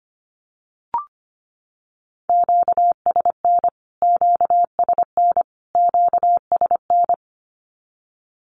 Morse code, heard as a keyed tone, about 25 wpm, 700 Hz. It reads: QSN QSN QSN
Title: Q-Codes Repeated 3x 25wpm